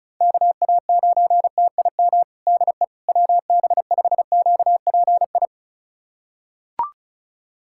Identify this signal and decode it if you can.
Morse code, heard as a keyed tone, about 34 wpm, 700 Hz. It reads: KA9TIM DE WB5QPI
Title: USA Callsign DE Callsign - 6 Characters - 35wpm